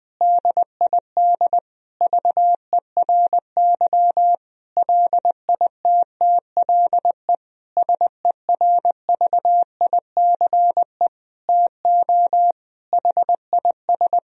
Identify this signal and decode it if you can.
Morse code, heard as a keyed tone, about 20 wpm, 700 Hz. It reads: DID VERY LITTLE SERVICE TO HIS
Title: Sentences from Top 1000 Words - Review Part 4 - 20wpm